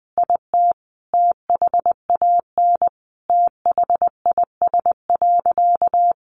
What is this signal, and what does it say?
Morse code, heard as a keyed tone, about 20 wpm, 700 Hz. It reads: IT THAN THIS.
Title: Sentences from Top 100 Words 20wpm